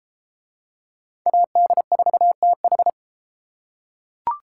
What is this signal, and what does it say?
Morse code, heard as a keyed tone, about 29 wpm, 700 Hz. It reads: AD4TH